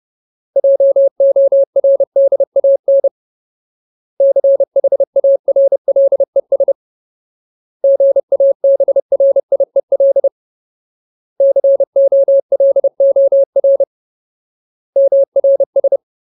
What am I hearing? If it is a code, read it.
Morse code, heard as a keyed tone, about 30 wpm, 550 Hz. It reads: JORDAN CHARLES GABRIEL COLOR MRS
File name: Ultimate Word List / 30WPM Ultimate